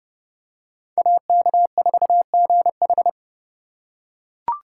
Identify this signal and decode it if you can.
Morse code, heard as a keyed tone, about 30 wpm, 700 Hz. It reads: AK4GH